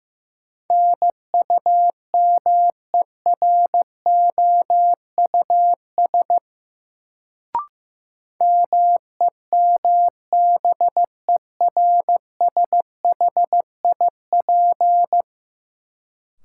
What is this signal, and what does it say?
Morse code, heard as a keyed tone, about 15 wpm, 700 Hz. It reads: NUMEROUS MEMBERSHIP
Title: Sentences from Top 1400 Words - New Word Drill Alt - 15wpm